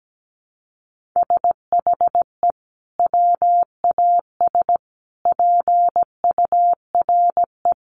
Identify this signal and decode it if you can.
Morse code, heard as a keyed tone, about 17 wpm, 700 Hz. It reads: SHE WAS PURE
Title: Sentences from Top 1500 Words 17wpm